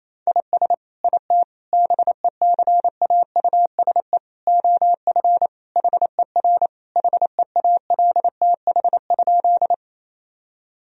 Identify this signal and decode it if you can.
Morse code, heard as a keyed tone, about 27 wpm, 700 Hz. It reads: IS IT BECAUSE OF HER HEALTH?
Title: Sentences from Top 1000 Words - Review Part 3 - 28wpm